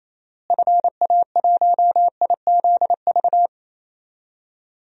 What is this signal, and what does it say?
Morse code, heard as a keyed tone, about 28 wpm, 700 Hz. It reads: FA1IZV
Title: DXCC-100 Call Signs - 6 Characters - Repeated 3x - 28wpm